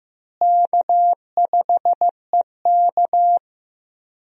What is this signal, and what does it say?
Morse code, heard as a keyed tone, about 15 wpm, 700 Hz. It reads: K5EK
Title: USA Call Signs - 4 Characters - Repeated 3x - 15wpm